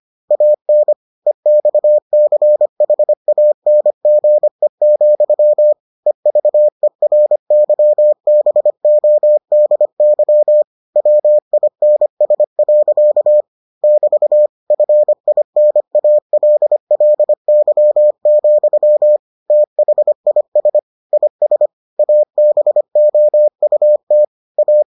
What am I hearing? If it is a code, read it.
Morse code, heard as a keyed tone, about 25 wpm, 600 Hz. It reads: AN EXCHANGE, EVERYBODY WINS. <BT> FINALLY, THIS IS ABOUT A